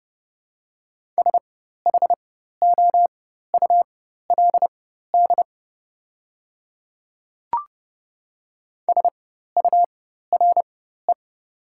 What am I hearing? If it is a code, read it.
Morse code, heard as a keyed tone, about 30 wpm, spaced as for 14 wpm, 700 Hz. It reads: SHOULD SURE
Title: 4x Character Spacing - Top 300 Words - 30wpm